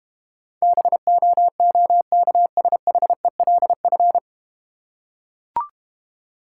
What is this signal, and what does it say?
Morse code, heard as a keyed tone, about 33 wpm, 700 Hz. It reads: BOOKSHELF